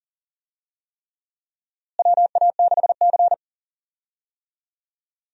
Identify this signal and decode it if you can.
Morse code, heard as a keyed tone, about 40 wpm, 700 Hz. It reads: WA6C
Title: USA Call Signs - 4 Characters - Repeated 3x - 40wpm